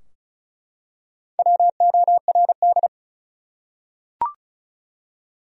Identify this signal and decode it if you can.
Morse code, heard as a keyed tone, about 34 wpm, 700 Hz. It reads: WORD
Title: Top 100 Words Repeated 35wpm